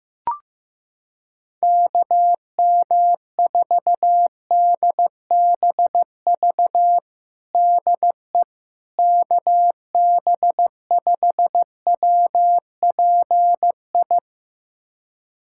Morse code, heard as a keyed tone, about 16 wpm, 700 Hz. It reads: KM4DBV DE KB5WPI
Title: USA Callsign DE Callsign - 6 Characters - 15wpm